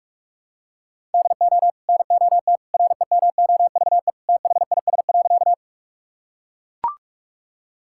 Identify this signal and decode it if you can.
Morse code, heard as a keyed tone, about 42 wpm, 700 Hz. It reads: DO NOT REMOVE THIS.
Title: Sentences from Top 1000 Words - Review Part 5 - 45wpm